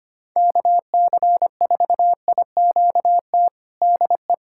Morse code, heard as a keyed tone, about 25 wpm, 700 Hz. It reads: KC4IQT DE
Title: USA Callsign DE Callsign 25wpm